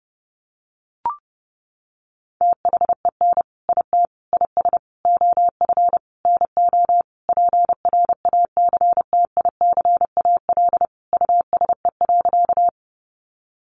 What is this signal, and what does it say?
Morse code, heard as a keyed tone, about 30 wpm, 700 Hz. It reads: THEN IT IS OF NO PRACTICAL USE.
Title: Sentences from Top 1400 Words 30wpm